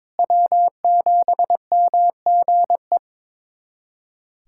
Morse code, heard as a keyed tone, about 22 wpm, 700 Hz. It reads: W7MGE
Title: USA Callsign DE Callsign - 5 Characters - 22wpm